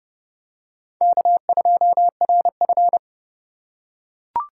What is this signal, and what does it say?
Morse code, heard as a keyed tone, about 28 wpm, 700 Hz. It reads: K2RF